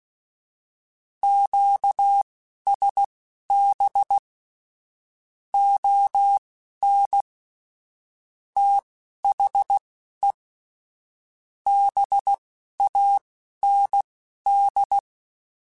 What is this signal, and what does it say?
Morse code, heard as a keyed tone, about 16 wpm, spaced as for 11 wpm, 800 Hz. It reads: QSB ON THE BAND
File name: QSOs / QSOs-mp3s / WD0BGP 1@13